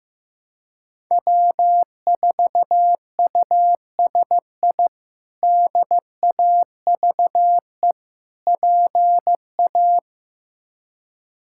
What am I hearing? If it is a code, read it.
Morse code, heard as a keyed tone, about 15 wpm, 700 Hz. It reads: W4USI DAVE PA
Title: K1USN SST QSO Practice - 15wpm